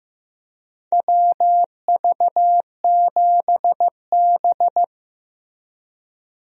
Morse code, heard as a keyed tone, about 15 wpm, 700 Hz. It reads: WV7B
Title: USA Call Signs - 4 Characters - Repeated 2x - 15wpm